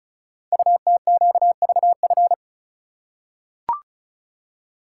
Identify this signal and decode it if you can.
Morse code, heard as a keyed tone, about 36 wpm, 700 Hz. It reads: UTQVF
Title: Code Groups - 5 Letters - 35wpm